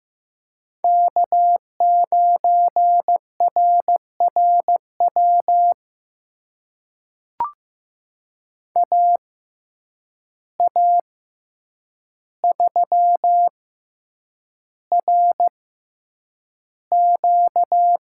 Morse code, heard as a keyed tone, about 15 wpm, spaced as for 8 wpm, 700 Hz. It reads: K9RRW AA3RQ
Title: USA Call Signs - 5 Characters - 6x Character Spacing - 15wpm